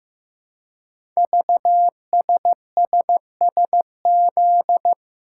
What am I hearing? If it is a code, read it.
Morse code, heard as a keyed tone, about 15 wpm, 700 Hz. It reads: VSSSZ
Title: Code Groups - 5 Letters - 15wpm